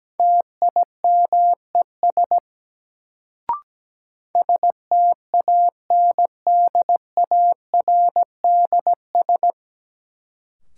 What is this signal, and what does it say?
Morse code, heard as a keyed tone, about 17 wpm, 700 Hz. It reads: TIMES STANDARDS
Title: Sentences from Top 1000 Words - Review Part 6 - New Word Drill - 17wpm